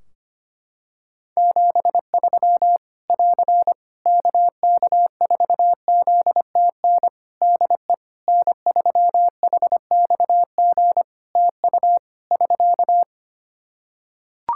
Morse code, heard as a keyed tone, about 25 wpm, 700 Hz. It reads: 73 <AR> KK4ZTN DE N3HXG TU <SK>
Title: Practice QSO - Exchange 4 - Ending - 25wpm